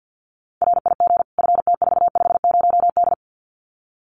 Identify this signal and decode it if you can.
Morse code, heard as a keyed tone, about 50 wpm, 700 Hz. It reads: RIG FT450D
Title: Common QSO Phrases 50wpm